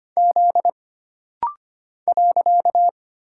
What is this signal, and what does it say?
Morse code, heard as a keyed tone, about 25 wpm, 700 Hz. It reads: Z .
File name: Single Character (Letter:Number:Punctuation:Prosign) - Rapid-Fire - 25wpm